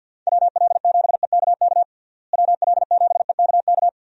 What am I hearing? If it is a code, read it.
Morse code, heard as a keyed tone, about 52 wpm, 700 Hz. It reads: WL7EKK WL7EKK
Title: USA Call Signs - 6 Characters - Repeated 3x - 50wpm